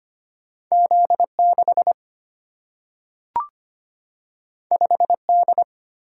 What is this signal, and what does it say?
Morse code, heard as a keyed tone, about 25 wpm, 700 Hz. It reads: Z6 5D
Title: Single-Letter Single-Number 25wpm